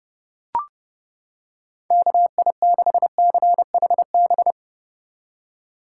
Morse code, heard as a keyed tone, about 30 wpm, 700 Hz. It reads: KI6CHB